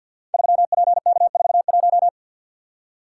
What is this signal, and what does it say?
Morse code, heard as a keyed tone, about 50 wpm, 700 Hz. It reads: 3PK41
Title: Code Groups - 5 Letter:Numbers - 50wpm